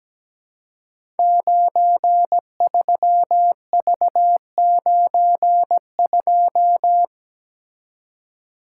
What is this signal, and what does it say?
Morse code, heard as a keyed tone, about 17 wpm, 700 Hz. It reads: 93V92
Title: Code Groups - 5 Letter:Numbers - 17wpm